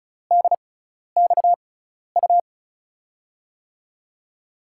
Morse code, heard as a keyed tone, about 36 wpm, spaced as for 12 wpm, 700 Hz. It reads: DXU
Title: USA Call Signs - 6 Characters - 6x Character Spacing - 35wpm